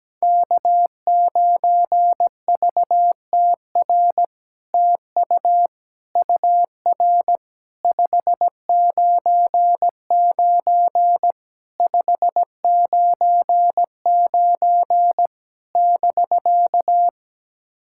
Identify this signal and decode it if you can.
Morse code, heard as a keyed tone, about 17 wpm, 700 Hz. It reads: K9VTR TU UR 599 599 <BK>
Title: SOTA QSO Practice 17wpm